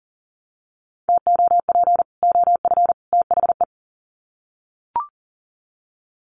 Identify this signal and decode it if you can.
Morse code, heard as a keyed tone, about 40 wpm, 700 Hz. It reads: TOP OF THE
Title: Sets of 3 Words 40wpm